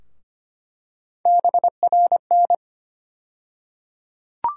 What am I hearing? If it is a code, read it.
Morse code, heard as a keyed tone, about 25 wpm, 700 Hz. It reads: BRN
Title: USA Call Sign Postfixes - 3 Letters 25wpm